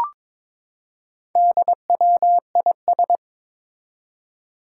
Morse code, heard as a keyed tone, about 22 wpm, 700 Hz. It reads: DWIS
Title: Four Challenging Letter-Number Combinations 22wpm